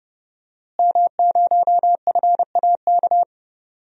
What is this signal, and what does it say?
Morse code, heard as a keyed tone, about 30 wpm, 700 Hz. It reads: M0FAK
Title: DXCC-100 Call Signs - 5 Characters - Repeated 3x - 30wpm